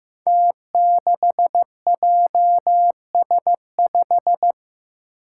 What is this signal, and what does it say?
Morse code, heard as a keyed tone, about 15 wpm, 700 Hz. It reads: T6JS5
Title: Five Challenging Letter-Number Combinations 15wpm